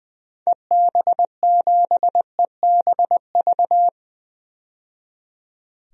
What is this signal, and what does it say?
Morse code, heard as a keyed tone, about 20 wpm, 700 Hz. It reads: EB7EBV
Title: DXCC-100 Call Signs - 6 Characters - 20wpm